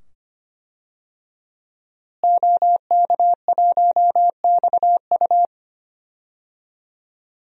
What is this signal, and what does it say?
Morse code, heard as a keyed tone, about 25 wpm, 700 Hz. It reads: OK1XU
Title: DXCC-100 Call Signs - 5 Characters - Repeated 2x - 25wpm